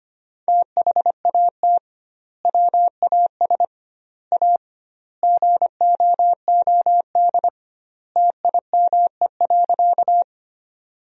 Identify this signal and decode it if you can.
Morse code, heard as a keyed tone, about 25 wpm, 700 Hz. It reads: THAT WAS A GOOD TIME.
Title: 2x Word-Spacing - Sentences from Top 200 Words - 25wpm